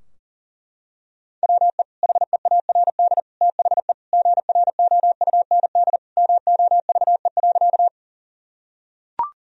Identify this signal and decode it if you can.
Morse code, heard as a keyed tone, about 40 wpm, 700 Hz. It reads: WE HEARD THE GROUND MOVE.
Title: Sentences from Top 1000 Words - Review Part 2 - 40wpm